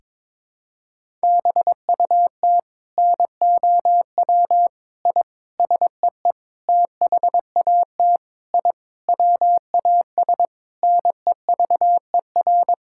Morse code, heard as a keyed tone, about 22 wpm, 700 Hz. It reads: BUT NOW I SEE THAT I WAS NEVER
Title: Sentences from Top 1000 Words - Review Part 6 - 22wpm